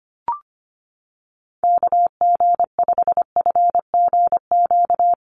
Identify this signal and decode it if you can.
Morse code, heard as a keyed tone, about 25 wpm, 700 Hz. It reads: KG5FGQ
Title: USA Call Signs Repeated 3x 25wpm